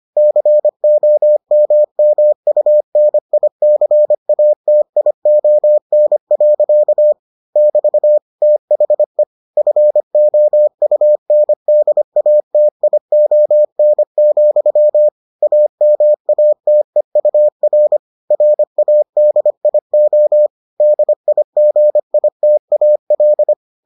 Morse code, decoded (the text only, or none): COMMUNICATION. <BT> THE FOUNDATION, AMATEUR RADIO DIGITAL